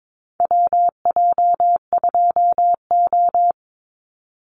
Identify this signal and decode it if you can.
Morse code, heard as a keyed tone, about 22 wpm, 700 Hz. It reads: WJ2O
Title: Call Signs - 22wpm